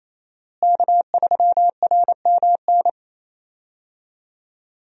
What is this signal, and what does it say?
Morse code, heard as a keyed tone, about 28 wpm, 700 Hz. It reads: K3RMN
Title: USA Call Signs - 5 Characters - Repeated 3x - 28wpm